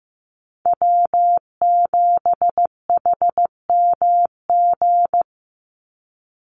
Morse code, heard as a keyed tone, about 15 wpm, 700 Hz. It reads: W7HMG
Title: USA Call Signs 15wpm